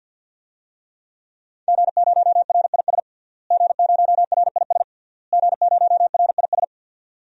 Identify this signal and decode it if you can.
Morse code, heard as a keyed tone, about 52 wpm, 700 Hz. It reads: G0RIS G0RIS G0RIS
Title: DXCC-100 Call Signs - 5 Characters - Repeated 3x - 50wpm